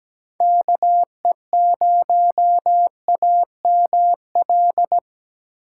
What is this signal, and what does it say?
Morse code, heard as a keyed tone, about 20 wpm, 700 Hz. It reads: KE0AML
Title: Speed Racing - USA 2x3 Call Signs 17wpm